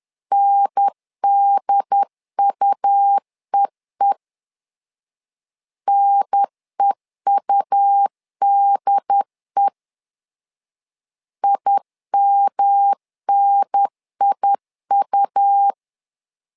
Morse code, heard as a keyed tone, about 10 wpm, 800 Hz. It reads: NDUEE NEUDE IMNIU
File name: LEZIONI / 3 MWRO / 3 M 10wpm 1828